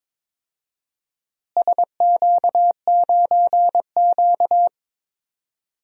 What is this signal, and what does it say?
Morse code, heard as a keyed tone, about 22 wpm, 700 Hz. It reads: SQ9Q
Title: DXCC-100 Call Signs - 4 Characters - Repeated 2x - 22wpm